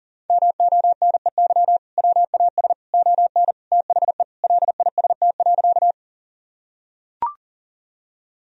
Morse code, heard as a keyed tone, about 40 wpm, 700 Hz. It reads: MONEY WAS ON THE LIST.
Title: Sentences from Top 1000 Words - Review Part 2 - 40wpm